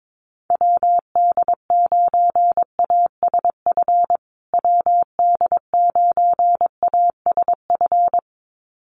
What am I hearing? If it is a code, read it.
Morse code, heard as a keyed tone, about 22 wpm, 700 Hz. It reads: WD9ASF WD9ASF
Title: USA Call Signs Repeated 2x 22wpm